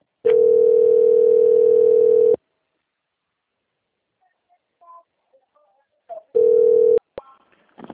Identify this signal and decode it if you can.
Ringback tone (North America), 440 + 480 Hz.